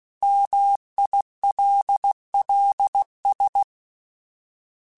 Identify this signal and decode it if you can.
Morse code, heard as a keyed tone, about 16 wpm, 800 Hz. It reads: MILLS
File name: fromSMs / FromSMs mp3s / Cities@15